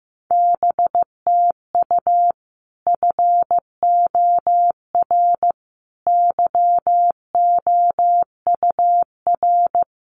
Morse code, heard as a keyed tone, about 15 wpm, 700 Hz. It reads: BTU FOR YOUR
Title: Common QSO Phrases 15wpm